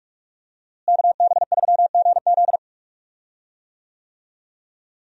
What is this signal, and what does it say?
Morse code, heard as a keyed tone, about 46 wpm, 700 Hz. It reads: KB3GZ
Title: USA Call Signs - 5 Characters - Repeated 2x - 45wpm